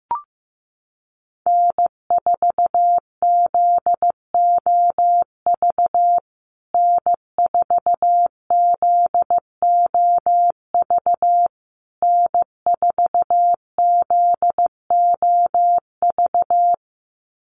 Morse code, heard as a keyed tone, about 15 wpm, 700 Hz. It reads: N4ZOV N4ZOV N4ZOV
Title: USA Call Signs Repeated 3x 15wpm